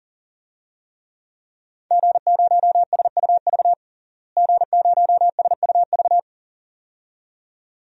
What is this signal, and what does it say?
Morse code, heard as a keyed tone, about 37 wpm, 700 Hz. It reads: G0SUV G0SUV
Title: DXCC-100 Call Signs - 5 Characters - Repeated 2x - 40wpm